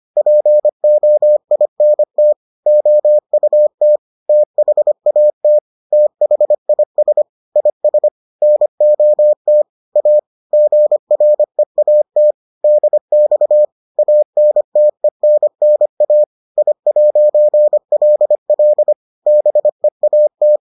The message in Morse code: POINT OUT THAT THIS IS NOT A GREAT DX ANTENNA I'LL BEAT

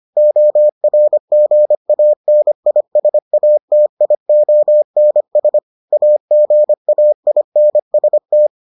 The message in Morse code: ORGANISATIONS AGAINST

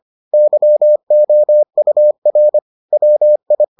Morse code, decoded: YOUR WI

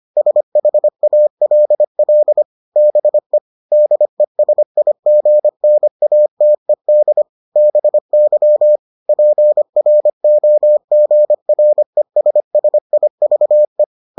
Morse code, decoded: SHALL BE DESIGNATED BY PROGRESSIVE

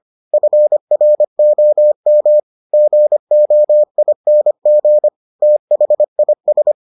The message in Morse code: FROM GOING THIS

